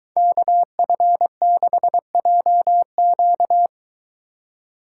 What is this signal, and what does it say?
Morse code, heard as a keyed tone, about 22 wpm, 700 Hz. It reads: KF6JQ